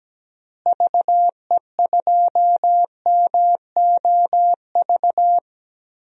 Morse code, heard as a keyed tone, about 17 wpm, 700 Hz. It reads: VE2MOV
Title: DXCC-100 Call Signs - 6 Characters - Repeated 3x - 17wpm